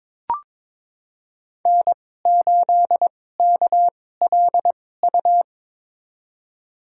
Morse code, heard as a keyed tone, about 22 wpm, spaced as for 16 wpm, 700 Hz. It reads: N8KLU
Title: USA Call Signs - 2x Character Spacing - 22wpm